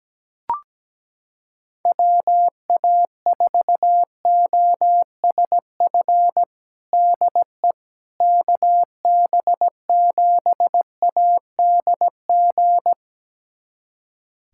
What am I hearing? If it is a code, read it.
Morse code, heard as a keyed tone, about 17 wpm, 700 Hz. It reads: WA4OSF DE KB7ADG
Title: USA Callsign DE Callsign - 6 Characters - 17wpm